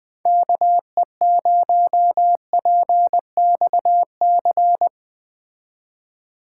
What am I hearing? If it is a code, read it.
Morse code, heard as a keyed tone, about 20 wpm, 700 Hz. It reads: KE0PXC
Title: Speed Racing - USA 2x3 Call Signs 17wpm